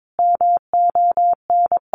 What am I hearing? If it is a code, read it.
Morse code, heard as a keyed tone, about 22 wpm, 700 Hz. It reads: MON
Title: Sentences from Top 1300 Words 22wpm